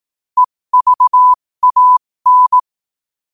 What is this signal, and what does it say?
Morse code, heard as a keyed tone, about 18 wpm, spaced as for 15 wpm, 1000 Hz. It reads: EVAN
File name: qotdfiles / 15wpm